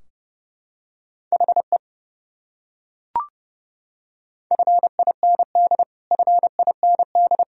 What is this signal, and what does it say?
Morse code, heard as a keyed tone, about 30 wpm, 700 Hz. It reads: HE FIND FIND
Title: Top 100 Words Repeated 30wpm